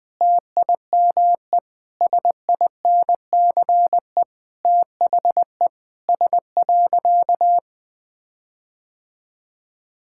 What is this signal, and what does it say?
Morse code, heard as a keyed tone, about 21 wpm, 700 Hz. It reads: TIME SINCE THE S.
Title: Sentences with 6-Word Phrases - Spoken Beginning - 20wpm